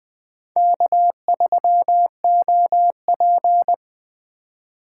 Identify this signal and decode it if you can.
Morse code, heard as a keyed tone, about 20 wpm, 700 Hz. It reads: K3OP
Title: Speed Racing - USA Call Signs 17wpm